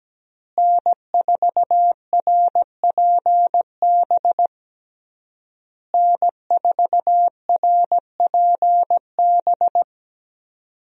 Morse code, heard as a keyed tone, about 17 wpm, 700 Hz. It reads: N4RPB N4RPB
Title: USA Call Signs - 5 Characters - Repeated 3x - 17wpm